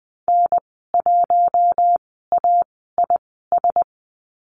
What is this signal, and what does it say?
Morse code, heard as a keyed tone, about 20 wpm, spaced as for 14 wpm, 700 Hz. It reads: N1AIS
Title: 069 Lesson 12g - Letter d - Call Signs 20wpm